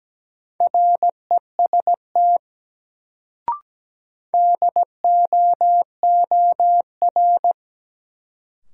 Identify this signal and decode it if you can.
Morse code, heard as a keyed tone, about 17 wpm, 700 Hz. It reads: REST DOOR
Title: Sets of 2 Words - New Word Drill - 17wpm